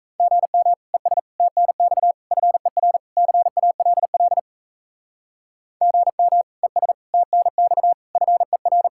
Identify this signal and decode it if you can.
Morse code, heard as a keyed tone, about 34 wpm, 700 Hz. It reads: GM ES TNX FER CALL GM ES TNX FER
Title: Speed-Racing Common QSO Phrases - 28wpm